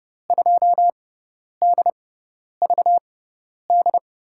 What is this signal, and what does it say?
Morse code, heard as a keyed tone, about 30 wpm, spaced as for 10 wpm, 700 Hz. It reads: 2DVD
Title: USA Call Signs - 5 Characters - 6x Character Spacing - 30wpm